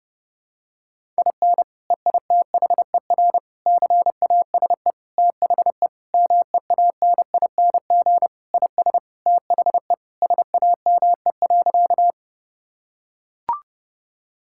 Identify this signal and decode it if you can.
Morse code, heard as a keyed tone, about 30 wpm, 700 Hz. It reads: IN EITHER CASE THE MEANING IS THE SAME.
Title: Sentences from Top 1000 Words - Review Part 6 - 30wpm